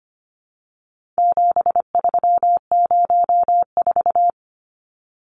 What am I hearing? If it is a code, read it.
Morse code, heard as a keyed tone, about 25 wpm, 700 Hz. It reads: 7304
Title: Numbers - 4-Digit 25wpm